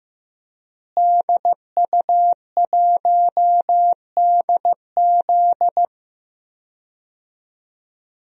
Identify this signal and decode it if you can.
Morse code, heard as a keyed tone, about 14 wpm, 700 Hz. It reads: DU1DZ
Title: DXCC-100 Call Signs - 5 Characters - Repeated 3x - 15wpm